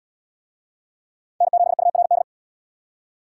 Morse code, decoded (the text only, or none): N7URD